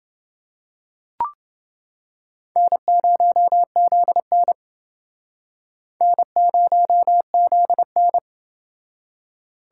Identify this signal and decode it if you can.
Morse code, heard as a keyed tone, about 28 wpm, 700 Hz. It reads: N0ZN N0ZN